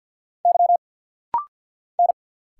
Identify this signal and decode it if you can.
Morse code, heard as a keyed tone, about 50 wpm, 690 Hz. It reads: Y N
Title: Single Letter-Number Alt - Rapid-Fire 50wpm